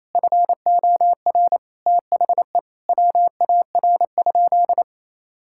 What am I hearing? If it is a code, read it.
Morse code, heard as a keyed tone, about 28 wpm, 700 Hz. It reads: FOR THE WAR?
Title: Sentences from Top 1000 Words - Review Part 5 - 28wpm